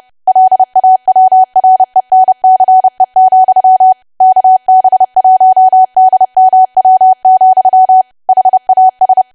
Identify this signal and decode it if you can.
Morse code, heard as a keyed tone, about 30 wpm, 750 Hz. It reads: LAWRENCE, KB1DMW, HAS
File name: cwa / W1AW-Level3 / 30WPM